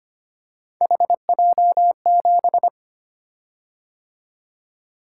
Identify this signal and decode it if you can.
Morse code, heard as a keyed tone, about 25 wpm, 700 Hz. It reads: HJ7
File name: Three Letter-Number Combinations - 25wpm